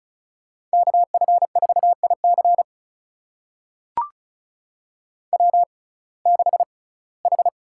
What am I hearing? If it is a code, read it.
Morse code, heard as a keyed tone, about 34 wpm, spaced as for 21 wpm, 700 Hz. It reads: KF4IC W6H
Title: USA Call Signs - 5 Characters - 6x Character Spacing - 35wpm